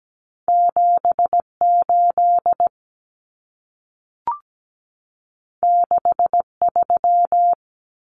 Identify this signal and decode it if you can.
Morse code, heard as a keyed tone, about 17 wpm, 700 Hz. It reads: 78 63
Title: Numbers - 2-Digit 17wpm